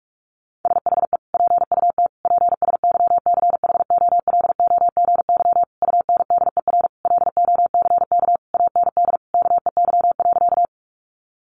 Morse code, heard as a keyed tone, about 44 wpm, 700 Hz. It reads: SHE PUT PSYCHOLOGY UNDER LOCK AND KEY.
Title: Sentences from Top 1400 Words 45wpm